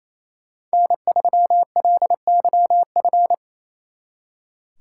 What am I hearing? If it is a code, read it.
Morse code, heard as a keyed tone, about 32 wpm, 700 Hz. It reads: N3LYF
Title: Speed Racing - USA 1x3 Call Signs 28wpm